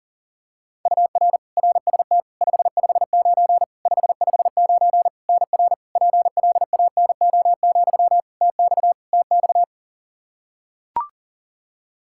Morse code, heard as a keyed tone, about 40 wpm, 700 Hz. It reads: UR RST 559 559 NR PLANO, TX TX
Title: Practice QSO - Exchange 1 - Beginning - 40wpm